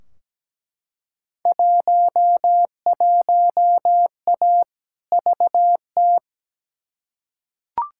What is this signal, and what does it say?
Morse code, heard as a keyed tone, about 17 wpm, 700 Hz. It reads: 11A VT
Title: Field Day Exchanges Encore - 17wpm